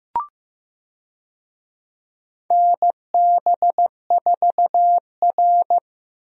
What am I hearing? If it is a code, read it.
Morse code, heard as a keyed tone, about 15 wpm, 700 Hz. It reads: NB4R
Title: USA Call Signs - 4 Characters - Repeated 2x - 15wpm